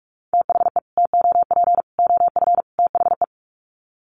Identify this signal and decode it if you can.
Morse code, heard as a keyed tone, about 44 wpm, 700 Hz. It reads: THE TOP OF THE
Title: Sets of 4 Words 45wpm